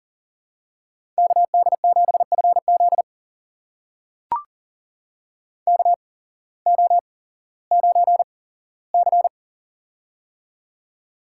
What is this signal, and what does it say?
Morse code, heard as a keyed tone, about 40 wpm, spaced as for 19 wpm, 700 Hz. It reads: KD7FZ KO9C
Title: USA Call Signs - 8x Character Spacing - 40wpm